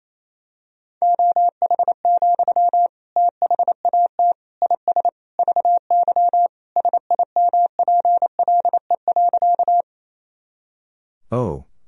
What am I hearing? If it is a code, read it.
Morse code, heard as a keyed tone, about 28 wpm, 700 Hz. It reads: OH, THAT IS VY SIMPLE.
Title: Sentences from Top 500 Words and Abbreviations 28wpm